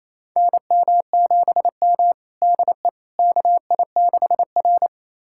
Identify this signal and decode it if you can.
Morse code, heard as a keyed tone, about 28 wpm, 700 Hz. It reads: NM7M DE KI6R
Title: USA Callsign DE Callsign - 4 Characters - 28wpm